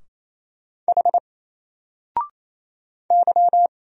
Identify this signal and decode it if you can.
Morse code, heard as a keyed tone, about 27 wpm, 700 Hz. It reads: H Y
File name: Single-Letter-Number-28wpm